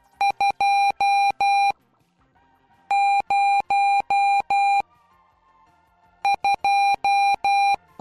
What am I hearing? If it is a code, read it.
Morse code, heard as a keyed tone, about 12 wpm, 800 Hz. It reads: 2 0 2